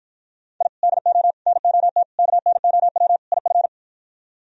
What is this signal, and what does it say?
Morse code, heard as a keyed tone, about 52 wpm, 700 Hz. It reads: I DO NOT KNOW IF